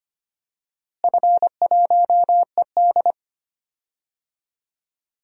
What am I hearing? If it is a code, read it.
Morse code, heard as a keyed tone, about 25 wpm, 700 Hz. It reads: F1ED
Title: DXCC-100 Call Signs - 4 Characters - Repeated 3x - 25wpm